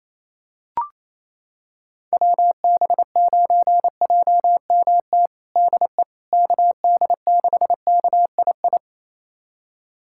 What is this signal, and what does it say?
Morse code, heard as a keyed tone, about 27 wpm, 700 Hz. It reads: WB9JMT DE KD6KII
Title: USA Callsign DE Callsign - 6 Characters - 28wpm